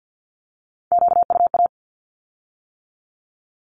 Morse code, heard as a keyed tone, about 50 wpm, 700 Hz. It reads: QUA
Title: Q-Codes with Spoken Meaning - 50wpm